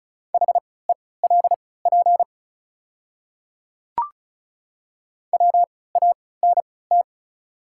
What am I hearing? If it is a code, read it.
Morse code, heard as a keyed tone, about 36 wpm, spaced as for 20 wpm, 700 Hz. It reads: HELP WANT
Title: 3x Character Spacing - Top 300 Words - 35wpm